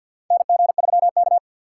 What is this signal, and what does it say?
Morse code, heard as a keyed tone, about 53 wpm, 700 Hz. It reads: NG3K
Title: USA Call Signs - 4x Character Spacing - 50wpm